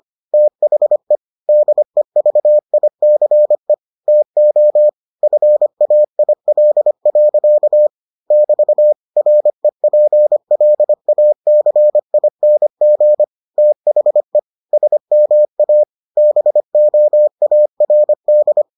Morse code, heard as a keyed tone, about 25 wpm, 600 Hz. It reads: THE DEVICE TO FAIL. <BT> REPLACING THE SMA BOARD